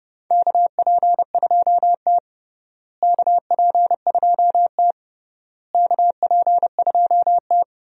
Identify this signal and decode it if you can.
Morse code, heard as a keyed tone, about 30 wpm, 700 Hz. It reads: KP2T KP2T KP2T
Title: USA Call Signs - 4 Characters - Repeated 3x - 30wpm